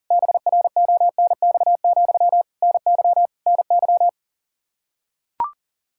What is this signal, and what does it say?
Morse code, heard as a keyed tone, about 40 wpm, 700 Hz. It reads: BRONX, NY NY
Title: Practice QSO - Exchange 1 - Beginning - 40wpm